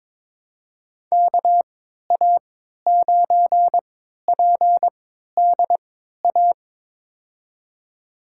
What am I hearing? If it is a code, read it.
Morse code, heard as a keyed tone, about 22 wpm, spaced as for 12 wpm, 700 Hz. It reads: KA9PDA
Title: USA Call Signs - 3x Character Spacing - 22wpm